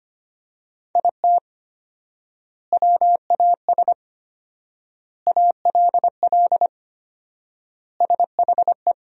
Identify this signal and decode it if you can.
Morse code, heard as a keyed tone, about 25 wpm, 700 Hz. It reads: IT WAS ALL SHE
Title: 4x Word-Spacing - Sentences from Top 400 Words - 25wpm